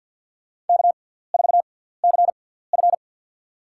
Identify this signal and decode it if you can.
Morse code, heard as a keyed tone, about 46 wpm, spaced as for 17 wpm, 690 Hz. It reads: K4CF
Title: USA Call Signs - 6x Character Spacing - 50wpm